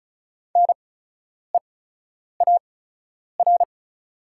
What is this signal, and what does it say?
Morse code, heard as a keyed tone, about 36 wpm, spaced as for 10 wpm, 700 Hz. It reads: NEAR
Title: 8x Character Spacing - Top 200 Words - 35wpm